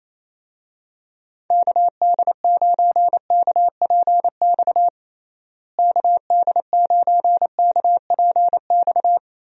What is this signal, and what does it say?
Morse code, heard as a keyed tone, about 28 wpm, 700 Hz. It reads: KD9KPX KD9KPX
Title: USA Call Signs - 6 Characters - Repeated 3x - 28wpm